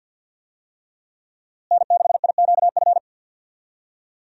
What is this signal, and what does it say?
Morse code, heard as a keyed tone, about 49 wpm, 700 Hz. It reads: N6IQF
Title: USA Call Signs - 5 Characters - Repeated 3x - 50wpm